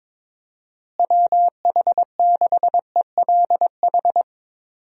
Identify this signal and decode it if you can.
Morse code, heard as a keyed tone, about 22 wpm, 700 Hz. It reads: WH6ELH
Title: USA Call Signs - 6 Characters - Repeated 2x - 22wpm